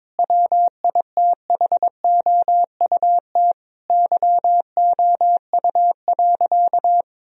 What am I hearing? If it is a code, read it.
Morse code, heard as a keyed tone, about 22 wpm, 700 Hz. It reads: WITHOUT YOU.
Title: Sentences from Top 1000 Words - Review Part 3 - 22wpm